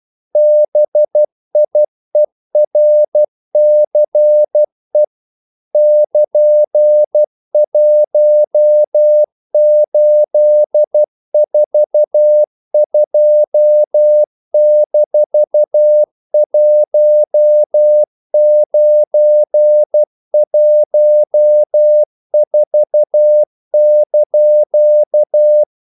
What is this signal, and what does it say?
Morse code, heard as a keyed tone, about 12 wpm, 600 Hz. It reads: BIERCE <KN>1842-1914)
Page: English quotations in CW (telegraphy - morse)